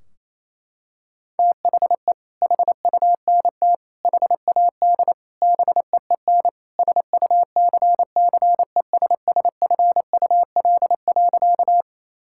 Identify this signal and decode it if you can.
Morse code, heard as a keyed tone, about 28 wpm, 700 Hz. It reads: THE HUNT HAD BEEN SUCCESSFUL.
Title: Sentences from Top 1600 Words 28wpm